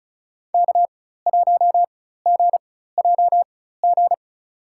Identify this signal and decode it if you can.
Morse code, heard as a keyed tone, about 36 wpm, spaced as for 16 wpm, 700 Hz. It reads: K1GJG
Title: USA Call Signs - 5 Characters - 4x Character Spacing - 35wpm